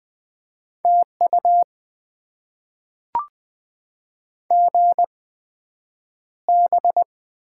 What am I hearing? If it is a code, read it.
Morse code, heard as a keyed tone, about 20 wpm, spaced as for 9 wpm, 700 Hz. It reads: TU GB
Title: Top 90 QSO Elements - 8x Character Spacing - 20wpm